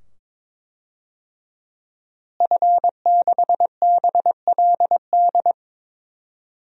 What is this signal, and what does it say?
Morse code, heard as a keyed tone, about 21 wpm, 700 Hz. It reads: F6BLD
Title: DXCC-100 Call Signs - 5 Characters - Repeated 2x - 22wpm